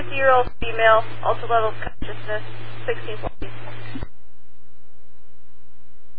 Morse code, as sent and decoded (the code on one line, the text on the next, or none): none